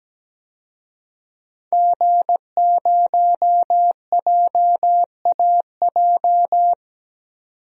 Morse code, heard as a keyed tone, about 17 wpm, 700 Hz. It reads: G0JAJ
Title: DXCC-100 Call Signs - 5 Characters - 17wpm